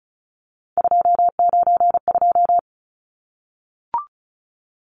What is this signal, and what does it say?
Morse code, heard as a keyed tone, about 34 wpm, 700 Hz. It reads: 292
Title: Numbers - 3-Digit 35wpm